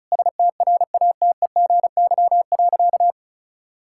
Morse code, heard as a keyed tone, about 36 wpm, 700 Hz. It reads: STRATEGY.
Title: Sentences from Top 1000 Words - Review Part 5 - 35wpm